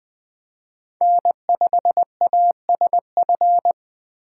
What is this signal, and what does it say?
Morse code, heard as a keyed tone, about 20 wpm, 700 Hz. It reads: N5ASF
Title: Speed Racing - USA 1x3 Call Signs 17wpm